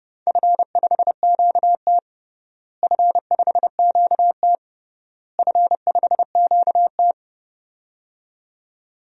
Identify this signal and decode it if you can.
Morse code, heard as a keyed tone, about 30 wpm, 700 Hz. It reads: F5QT F5QT F5QT
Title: DXCC-100 Call Signs - 4 Characters - Repeated 3x - 30wpm